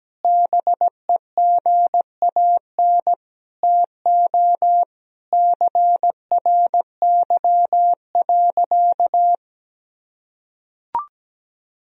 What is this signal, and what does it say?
Morse code, heard as a keyed tone, about 17 wpm, 700 Hz. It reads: BEGAN TO CRY.
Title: Sentences from Top 1000 Words - Review Part 2 - 17wpm